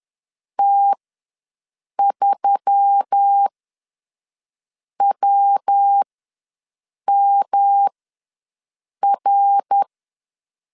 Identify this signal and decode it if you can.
Morse code, heard as a keyed tone, about 10 wpm, spaced as for 5 wpm, 800 Hz. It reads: T3WMR
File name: LEZIONI / 5 LBVC / 5 L ws 10wpm 3636